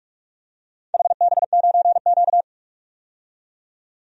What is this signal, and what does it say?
Morse code, heard as a keyed tone, about 44 wpm, 700 Hz. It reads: HB9Q
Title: DXCC-100 Call Signs - 4 Characters - 45wpm